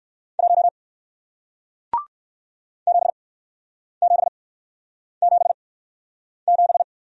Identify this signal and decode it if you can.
Morse code, heard as a keyed tone, about 57 wpm, spaced as for 9 wpm, 700 Hz. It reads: 1 7777